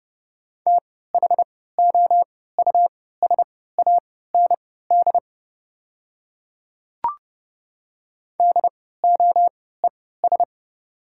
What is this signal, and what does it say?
Morse code, heard as a keyed tone, about 30 wpm, spaced as for 17 wpm, 700 Hz. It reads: THOUSAND DOES
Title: 3x Character Spacing - Top 500 Words - 30wpm